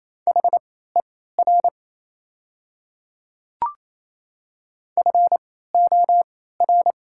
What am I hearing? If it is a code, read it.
Morse code, heard as a keyed tone, about 28 wpm, spaced as for 16 wpm, 700 Hz. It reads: HER FOR
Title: 3x Character Spacing - Top 100 Words - 28wpm